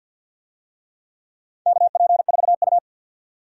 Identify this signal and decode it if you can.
Morse code, heard as a keyed tone, about 46 wpm, 700 Hz. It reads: KP4U
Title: DXCC-100 Call Signs - 4 Characters - 50wpm